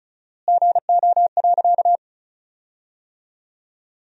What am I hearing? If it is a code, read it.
Morse code, heard as a keyed tone, about 34 wpm, 700 Hz. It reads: GO.
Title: 4x Word-Spacing - Sentences from Top 100 Words - 35wpm